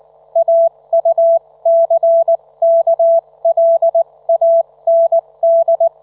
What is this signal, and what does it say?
Morse code, heard as a keyed tone, about 19 wpm, spaced as for 17 wpm, 670 Hz. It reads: AUCKLAND
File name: Trivia